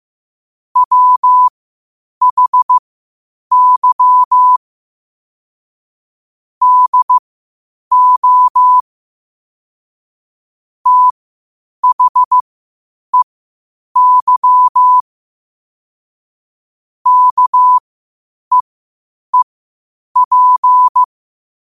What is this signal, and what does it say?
Morse code, heard as a keyed tone, about 15 wpm, spaced as for 8 wpm, 1000 Hz. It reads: WHY DO THEY KEEP
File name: qotdfiles / 8wpm